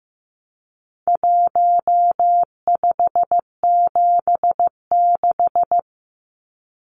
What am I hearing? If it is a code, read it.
Morse code, heard as a keyed tone, about 14 wpm, 700 Hz. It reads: 1576
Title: Numbers - Contest Style - 4-Digit 15wpm